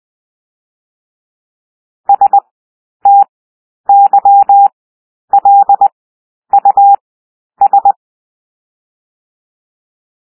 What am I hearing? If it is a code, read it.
Morse code, heard as a keyed tone, about 20 wpm, 800 Hz. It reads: S T Y L U S
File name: cwmp3 / W1 SOUND WORDS 20 wpm